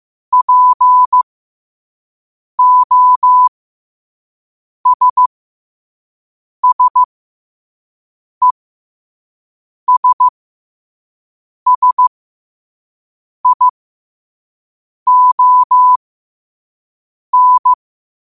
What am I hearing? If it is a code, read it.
Morse code, heard as a keyed tone, about 15 wpm, spaced as for 5 wpm, 1000 Hz. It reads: POSSESSION
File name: qotdfiles / 5wpm